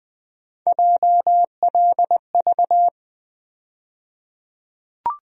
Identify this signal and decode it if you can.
Morse code, heard as a keyed tone, about 20 wpm, 700 Hz. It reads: JLV
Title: USA Call Sign Postfixes - 3 Letters 20wpm